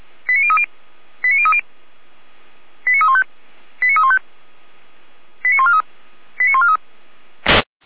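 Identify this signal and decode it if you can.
Five-tone selective call (ZVEI), heard as quick runs of five tones, tones of about 70 ms. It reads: 89030 89030 89315 89315 89143 89143
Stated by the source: ZVEI five-tone call 89501, received by SDR.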